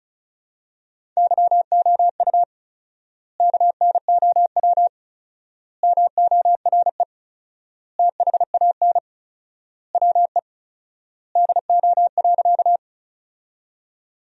Morse code, heard as a keyed tone, about 34 wpm, 700 Hz. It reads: YOU KNOW MORE THAN WE DO.
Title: 4x Word-Spacing - Sentences from Top 100 Words - 35wpm